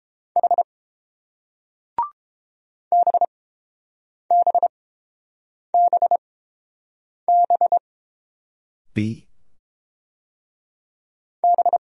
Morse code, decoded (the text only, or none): H BBBB B